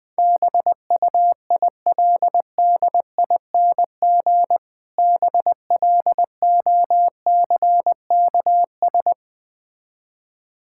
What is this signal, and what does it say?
Morse code, heard as a keyed tone, about 20 wpm, 700 Hz. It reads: BUILDING BLOCKS
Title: Difficult Words with Hints - 20wpm